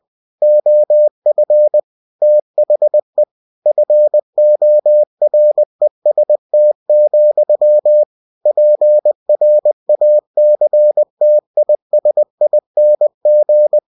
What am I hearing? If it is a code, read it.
Morse code, heard as a keyed tone, about 20 wpm, 600 Hz. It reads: OF THE FOREST, PRACTISING
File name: cw-books / winnie-the-pooh / winnie-20wpm-page-0071